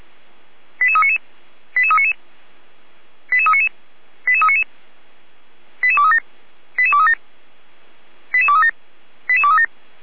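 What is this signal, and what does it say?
Five-tone selective call (ZVEI), heard as quick runs of five tones, tones of about 70 ms. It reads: 88399 88399 88399 88399 88237 88237 88237 88237
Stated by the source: ZVEI five-tone call received by SDR, archived by fire brigade Feuerwehr Orlamünde.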